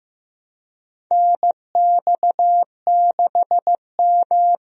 Morse code, heard as a keyed tone, about 15 wpm, 700 Hz. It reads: NX6M
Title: USA Call Signs - 4 Characters - Repeated 2x - 15wpm